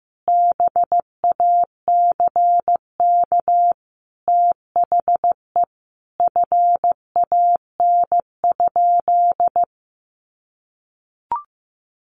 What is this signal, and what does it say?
Morse code, heard as a keyed tone, about 15 wpm, 700 Hz. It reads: BACK THE FAN?
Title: Sentences from Top 1300 Words 15wpm